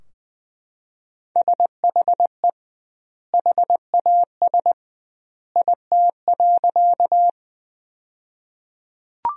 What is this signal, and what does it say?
Morse code, heard as a keyed tone, about 21 wpm, 700 Hz. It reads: SHE HAS IT.
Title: 2x Word-Spacing - Sentences from Top 100 Words - 20wpm